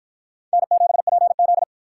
Morse code, heard as a keyed tone, about 48 wpm, 700 Hz. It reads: N7PZ